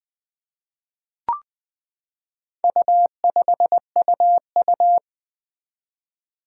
Morse code, heard as a keyed tone, about 20 wpm, 700 Hz. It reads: U5UU
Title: Four Challenging Letter-Number Combinations 20wpm